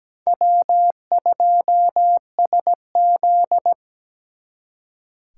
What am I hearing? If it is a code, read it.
Morse code, heard as a keyed tone, about 20 wpm, 700 Hz. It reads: W2SZ
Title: Speed Racing - USA 1x2 Call Signs 17wpm